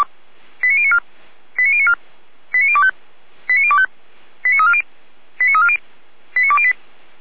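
Five-tone selective call (ZVEI), heard as quick runs of five tones, tones of about 70 ms. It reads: 89094 89094 89025 89025 89340 89340 89208
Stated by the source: ZVEI five-tone call received on SDR.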